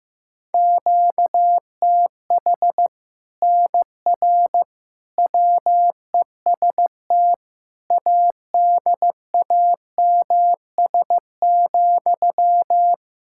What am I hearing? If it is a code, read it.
Morse code, heard as a keyed tone, about 15 wpm, 700 Hz. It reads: QTH NR WEST ADAMS,
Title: Practice QSO - Exchange 1 - Beginning - 15wpm